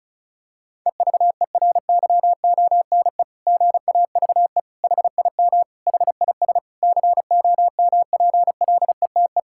Morse code, heard as a keyed tone, about 34 wpm, 700 Hz. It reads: EVERYONE GAVE HIM HIS COMPLETE
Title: Sentences from Top 1000 Words - Review Part 6 - 35wpm